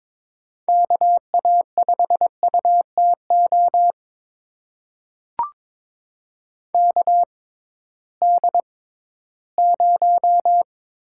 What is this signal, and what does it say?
Morse code, heard as a keyed tone, about 22 wpm, spaced as for 14 wpm, 700 Hz. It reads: KA5UTO KD0
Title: USA Call Signs - 6 Characters - 6x Character Spacing - 22wpm